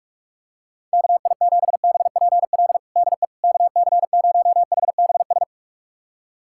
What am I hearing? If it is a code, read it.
Morse code, heard as a keyed tone, about 41 wpm, 700 Hz. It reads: KI7BPL DE KC0HBS
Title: USA Callsign DE Callsign - 6 Characters - 45wpm